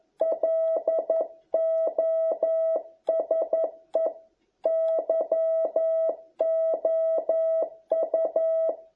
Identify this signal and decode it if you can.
Morse code, heard as a keyed tone, about 11 wpm, 660 Hz. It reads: LOSE YOU